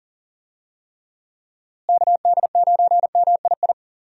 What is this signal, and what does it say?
Morse code, heard as a keyed tone, about 40 wpm, 700 Hz. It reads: KD9MII
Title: USA Call Signs - 6 Characters - Repeated 2x - 40wpm